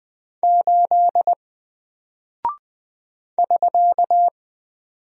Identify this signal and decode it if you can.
Morse code, heard as a keyed tone, about 20 wpm, 700 Hz. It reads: 8 <SK>
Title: Single Character (Letter:Number:Punctuation:Prosign) 20wpm